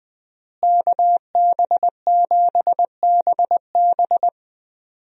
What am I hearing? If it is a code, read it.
Morse code, heard as a keyed tone, about 20 wpm, 700 Hz. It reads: KB7BB
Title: USA Call Signs - 5 Characters - Repeated 2x - 20wpm